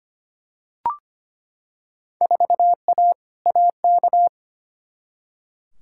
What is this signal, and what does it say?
Morse code, heard as a keyed tone, about 25 wpm, 700 Hz. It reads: 4A AK
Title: Field Day Exchanges - 25wpm